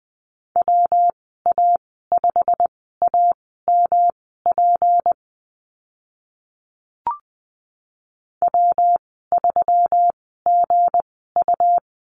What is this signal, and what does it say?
Morse code, heard as a keyed tone, about 20 wpm, spaced as for 14 wpm, 700 Hz. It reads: WA5AMP W3GU
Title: 167 Lesson 27g - Number 7 - Call Signs 20wpm